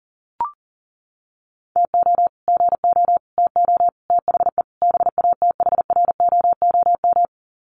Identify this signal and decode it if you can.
Morse code, heard as a keyed tone, about 40 wpm, 700 Hz. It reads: TO GO TO THE BATHROOM
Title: Sets of 5 Words Encore 40wpm